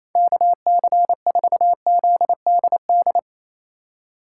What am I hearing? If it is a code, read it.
Morse code, heard as a keyed tone, about 28 wpm, 700 Hz. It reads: KC4ZDD
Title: USA Call Signs - 6 Characters - 2x Character Spacing - 28wpm